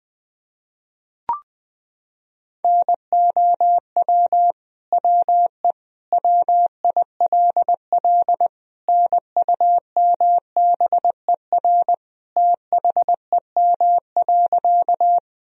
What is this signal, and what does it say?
Morse code, heard as a keyed tone, about 21 wpm, 700 Hz. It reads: NOW WE WILL NUMBER THEM.
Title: Sentences from Top 1000 Words - Review Part 1 - 20wpm